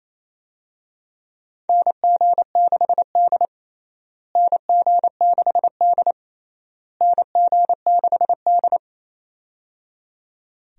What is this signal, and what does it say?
Morse code, heard as a keyed tone, about 28 wpm, 700 Hz. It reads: NG6D NG6D NG6D
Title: USA Call Signs - 4 Characters - Repeated 3x - 28wpm